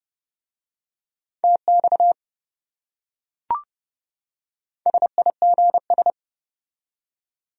Morse code, heard as a keyed tone, about 28 wpm, 700 Hz. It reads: TX SIGS